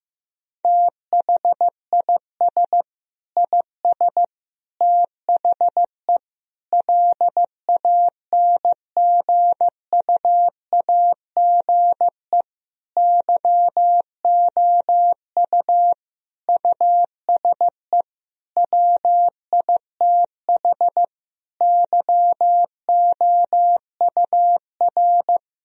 Morse code, heard as a keyed tone, about 15 wpm, 700 Hz. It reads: THIS IS THE LANGUAGE YOU USE WITH YOUR
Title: Sentences from Top 1000 Words - Review Part 3 - 15wpm